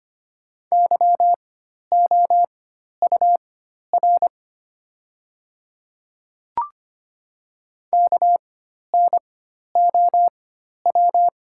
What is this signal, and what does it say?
Morse code, heard as a keyed tone, about 25 wpm, spaced as for 12 wpm, 700 Hz. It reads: YOUR KNOW
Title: 4x Character Spacing - Top 200 Words - 25wpm